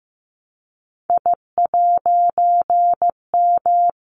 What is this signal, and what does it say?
Morse code, heard as a keyed tone, about 15 wpm, 700 Hz. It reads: I'M
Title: A Sound Of Thunder 15wpm